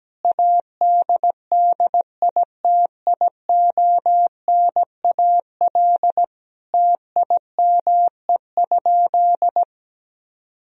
Morse code, heard as a keyed tone, about 17 wpm, 700 Hz. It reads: ADDITIONAL TIME?
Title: Sentences from Top 1000 Words - Review Part 4 - 17wpm